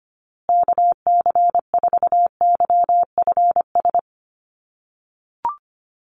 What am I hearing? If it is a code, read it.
Morse code, heard as a keyed tone, about 25 wpm, 700 Hz. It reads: KC4YFS
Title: USA Call Signs Repeated 3x 25wpm